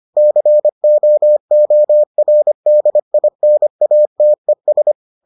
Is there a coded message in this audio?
Yes: Morse code from 0.2 to 4.9 s, heard as a keyed tone, about 25 wpm, 600 Hz. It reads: COORDINATES